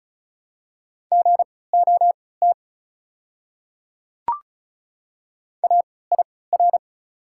Morse code, heard as a keyed tone, about 36 wpm, spaced as for 20 wpm, 700 Hz. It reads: GOT AIR
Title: 3x Character Spacing - Top 300 Words - 35wpm